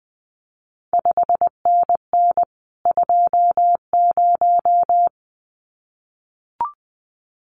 Morse code, heard as a keyed tone, about 19 wpm, 700 Hz. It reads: 5NN 20
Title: CQ WW DX Contest Exchange Encore - 20wpm